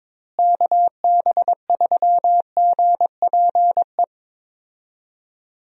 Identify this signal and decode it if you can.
Morse code, heard as a keyed tone, about 22 wpm, 700 Hz. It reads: KB3GPE
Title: USA Call Signs - 6 Characters - Repeated 2x - 22wpm